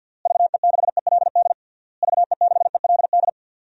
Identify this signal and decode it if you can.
Morse code, heard as a keyed tone, about 48 wpm, 700 Hz. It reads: VE6ELD VE6ELD
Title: DXCC-100 Call Signs - 6 Characters - Repeated 3x - 50wpm